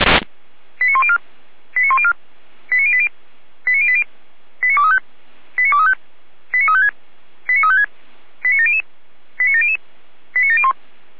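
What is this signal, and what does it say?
Five-tone selective call (ZVEI), heard as quick runs of five tones, tones of about 70 ms. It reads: 89104 89104 89080 89080 89235 89235 89356 89356 89700 89700 89701
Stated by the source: Received on SDR, ZVEI five-tone call, address 89305, from the audio server of Feuerwehr Orlamünde.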